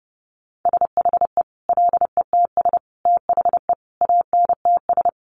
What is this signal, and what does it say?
Morse code, heard as a keyed tone, about 30 wpm, 700 Hz. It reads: SHE LETS THE ANTS
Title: Easy Phrases 30wpm